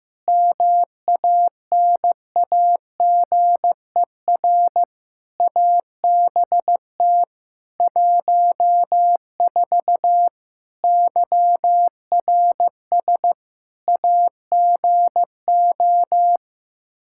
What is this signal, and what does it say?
Morse code, heard as a keyed tone, about 15 wpm, 700 Hz. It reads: MANAGER ABT 14 YRS AGO
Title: Practice QSO - Exchange 3 - More Info - 15wpm